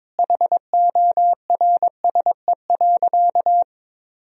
Morse code, heard as a keyed tone, about 22 wpm, 700 Hz. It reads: HORSE.
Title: Sentences from Top 1000 Words - Review Part 4 - 22wpm